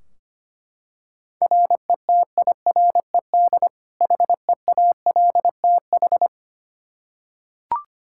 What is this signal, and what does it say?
Morse code, heard as a keyed tone, about 25 wpm, 700 Hz. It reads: RETIRED HEALTH
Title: Practice QSO - Exchange 3 - More Info - 25wpm